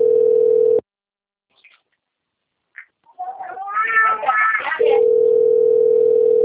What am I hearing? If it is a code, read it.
Ringback tone (North America), 440 + 480 Hz.